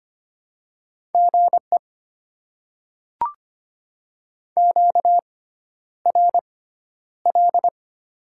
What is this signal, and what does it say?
Morse code, heard as a keyed tone, about 25 wpm, spaced as for 11 wpm, 700 Hz. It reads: GE QRL
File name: Top 20 QSO Elements - 6x Character Spacing - 25wpm